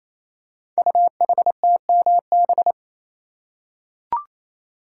Morse code, heard as a keyed tone, about 29 wpm, 700 Hz. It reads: UHTMB
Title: Code Groups - 5 Letter:Numbers - 28wpm